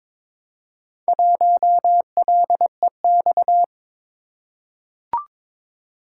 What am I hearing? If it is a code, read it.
Morse code, heard as a keyed tone, about 22 wpm, 700 Hz. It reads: 1LEX
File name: practice / 22wpm / Four Letter-Number Combinations - 22wpm